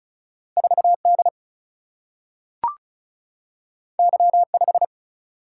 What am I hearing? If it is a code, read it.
Morse code, heard as a keyed tone, about 36 wpm, 700 Hz. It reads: 4D Y5
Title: Single-Letter Single-Number 35wpm